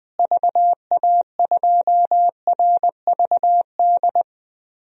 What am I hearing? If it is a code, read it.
Morse code, heard as a keyed tone, about 20 wpm, 700 Hz. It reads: VA2RVD
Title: DXCC-100 Call Signs - 6 Characters - Repeated 2x - 20wpm